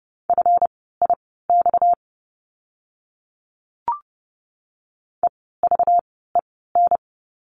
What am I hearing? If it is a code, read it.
Morse code, heard as a keyed tone, about 30 wpm, spaced as for 17 wpm, 700 Hz. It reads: FIX EVEN
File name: course / 30:10wpm / 230 Lesson 37f - Letter x - Words 30wpm